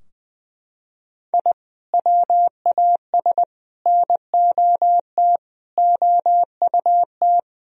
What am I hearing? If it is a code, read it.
Morse code, heard as a keyed tone, about 20 wpm, 700 Hz. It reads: I WAS NOT OUT
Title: Sentences from Top 1000 Words - Review Part 2 - 20wpm